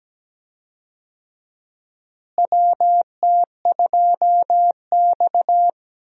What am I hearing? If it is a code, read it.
Morse code, heard as a keyed tone, about 17 wpm, 700 Hz. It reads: WT2X
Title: USA Call Signs - 4 Characters - Repeated 2x - 17wpm